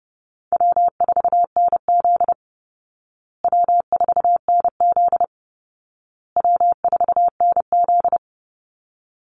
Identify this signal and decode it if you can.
Morse code, heard as a keyed tone, about 30 wpm, 700 Hz. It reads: W4NZ W4NZ W4NZ
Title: Easy Call Signs Repeated 3x - 30wpm